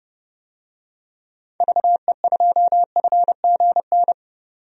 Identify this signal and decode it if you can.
Morse code, heard as a keyed tone, about 32 wpm, 700 Hz. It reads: VE2FGN
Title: DXCC-100 Call Signs - 6 Characters - 30wpm